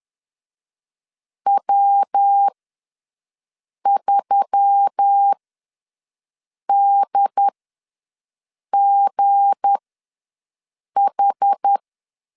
Morse code, heard as a keyed tone, about 10 wpm, spaced as for 5 wpm, 800 Hz. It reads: W3DGH
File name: LEZIONI / 7 GF9Z / 7 GF ws 10wpm 3918